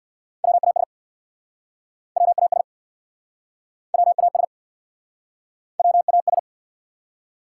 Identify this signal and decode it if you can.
Morse code, heard as a keyed tone, about 69 wpm, 700 Hz. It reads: WAS WAS WAS WAS